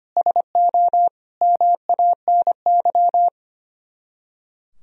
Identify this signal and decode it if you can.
Morse code, heard as a keyed tone, about 31 wpm, 700 Hz. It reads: SO MANY
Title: Speed-Racing - Sets of 4 Words 25wpm